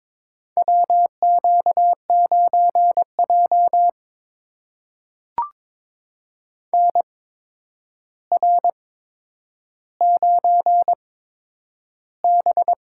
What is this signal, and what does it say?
Morse code, heard as a keyed tone, about 22 wpm, spaced as for 9 wpm, 700 Hz. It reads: WQ9J NR9B
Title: USA Call Signs - 4 Characters - 8x Character Spacing - 22wpm